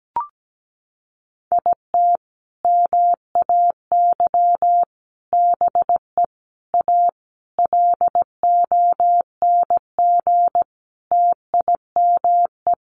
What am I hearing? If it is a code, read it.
Morse code, heard as a keyed tone, about 17 wpm, 700 Hz. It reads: IT MAY BE A LONG TIME
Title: Sentences from Top 1100 Words 17wpm